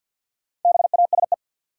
Morse code, heard as a keyed tone, about 51 wpm, 700 Hz. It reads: BASE
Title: Sentences from Top 1600 Words - New Word Drill Alt - 50wpm